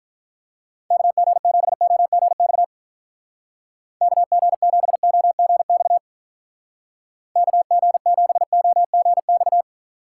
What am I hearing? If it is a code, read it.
Morse code, heard as a keyed tone, about 46 wpm, 700 Hz. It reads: KG7OGX KG7OGX KG7OGX